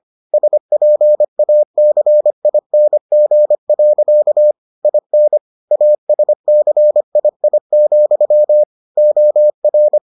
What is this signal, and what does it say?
Morse code, heard as a keyed tone, about 25 wpm, 600 Hz. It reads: SPACING. IN ASCII, OR